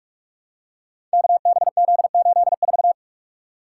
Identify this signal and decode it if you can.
Morse code, heard as a keyed tone, about 44 wpm, 700 Hz. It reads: KBZ84
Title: Code Groups - 5 Letter:Numbers - 45wpm